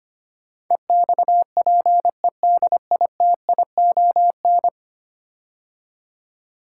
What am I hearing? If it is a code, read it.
Morse code, heard as a keyed tone, about 25 wpm, 700 Hz. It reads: EXPEDITION
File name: 10 Letter Words - First 3 Letters Given - 25wpm